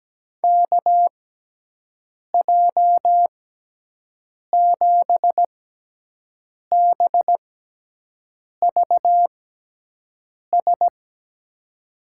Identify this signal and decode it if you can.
Morse code, heard as a keyed tone, about 17 wpm, spaced as for 6 wpm, 700 Hz. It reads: KJ7BVS
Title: USA Call Signs - 6x Character Spacing - 17wpm